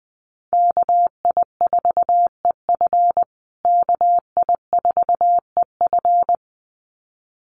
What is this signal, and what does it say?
Morse code, heard as a keyed tone, about 20 wpm, 700 Hz. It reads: KI4EF KI4EF
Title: USA Call Signs Repeated 2x 20wpm